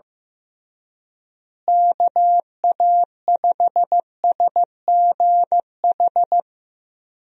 Morse code, heard as a keyed tone, about 15 wpm, 700 Hz. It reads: KA5SGH
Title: USA Call Signs - 6 Characters - Repeated 2x - 15wpm